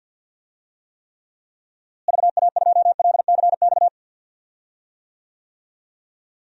Morse code, heard as a keyed tone, about 50 wpm, 700 Hz. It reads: VA2LCX
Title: DXCC-100 Call Signs - 6 Characters - Repeated 2x - 50wpm